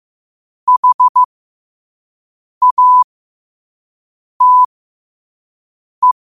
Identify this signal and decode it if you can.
Morse code, heard as a keyed tone, about 15 wpm, spaced as for 5 wpm, 1000 Hz. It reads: HATE
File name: qotdfiles / 5wpm